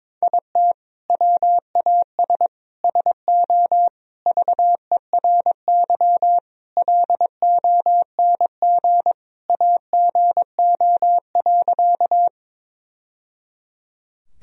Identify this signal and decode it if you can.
Morse code, heard as a keyed tone, about 22 wpm, 700 Hz. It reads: IT WAS SO VERY LONG AGO.
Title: Sentences from Top 1000 Words - Review Part 3 - 22wpm